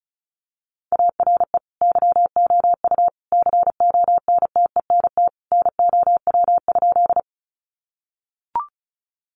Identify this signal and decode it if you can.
Morse code, heard as a keyed tone, about 34 wpm, 700 Hz. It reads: ARE YOU CONTENT NOW?
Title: Sentences from Top 900 Words 35wpm